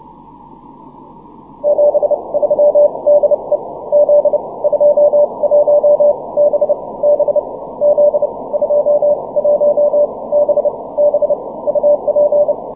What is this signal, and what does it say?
Morse code, heard as a keyed tone, about 29 wpm, spaced as for 22 wpm, 590 Hz, with radio noise. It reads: 73 DE Z21BB Z21BB UP
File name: wp / z21bb 12 cw 20 mar 2010 0856z